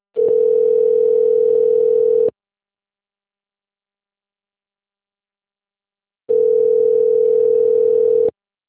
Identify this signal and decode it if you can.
Ringback tone (North America), 440 + 480 Hz.
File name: rg-1600-unknown- - - .33307